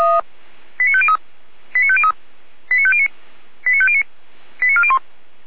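Five-tone selective call (ZVEI), heard as quick runs of five tones, tones of about 80 ms. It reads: 89503 89503 89509 89509 89401
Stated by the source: Five-tone ZVEI selective call received by SDR.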